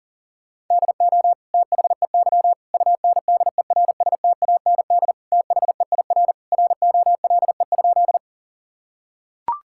Morse code, heard as a keyed tone, about 37 wpm, 700 Hz. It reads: DO THEY UNDERSTAND THEIR ROLE?
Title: Sentences from Top 1000 Words - Review Part 6 - 40wpm